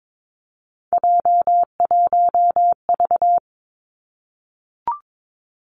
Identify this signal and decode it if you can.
Morse code, heard as a keyed tone, about 22 wpm, 700 Hz. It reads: J1V
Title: Three Challenging Letter-Number Combinations 22wpm